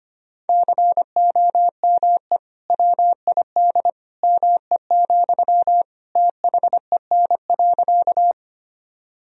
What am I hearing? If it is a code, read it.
Morse code, heard as a keyed tone, about 25 wpm, 700 Hz. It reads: COME WID ME, THEN.
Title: Sentences from Top 200 Words and Abbreviations 25wpm